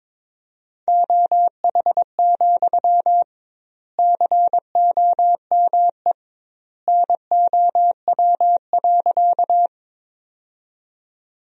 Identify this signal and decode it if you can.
Morse code, heard as a keyed tone, about 22 wpm, 700 Hz. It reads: OH, COME NOW.
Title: 2x Word-Spacing - Sentences from Top 500 Words - 22wpm